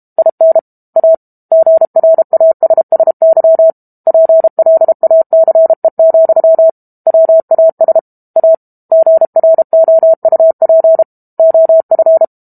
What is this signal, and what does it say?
Morse code, heard as a keyed tone, about 32 wpm, 650 Hz. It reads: IN A GRASSY PLACE, WAS A GROUP OF
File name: Book Two / 06-Chapter 6 - 32 WPM